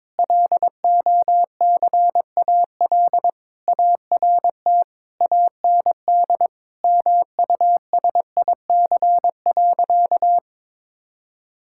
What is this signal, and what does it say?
Morse code, heard as a keyed tone, about 22 wpm, 700 Hz. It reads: LOCAL ART AND MUSIC.
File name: Difficult Sentences with Hints - 22wpm